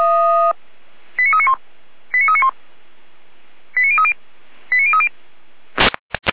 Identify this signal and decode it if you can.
Five-tone selective call (ZVEI), heard as quick runs of five tones, tones of about 70 ms. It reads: 89391 89391 89030 89030